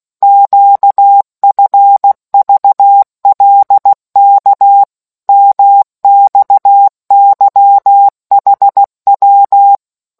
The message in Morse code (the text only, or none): QFVLK MXYHW